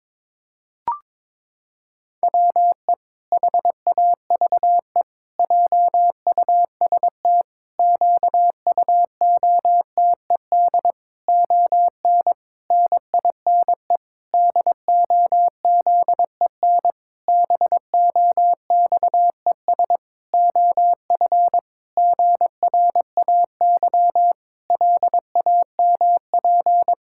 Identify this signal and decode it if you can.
Morse code, heard as a keyed tone, about 22 wpm, 700 Hz. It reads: WE HAVE JUST QUOTED ON NINE DOZEN BOXES OF GRAY LAMP
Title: Pangram Challenge v2 - 22wpm